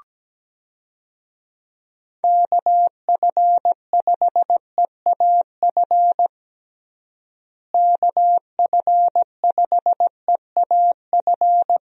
Morse code, heard as a keyed tone, about 17 wpm, 700 Hz. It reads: KF5EAF KF5EAF
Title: USA Call Signs - 6 Characters - Repeated 2x - 17wpm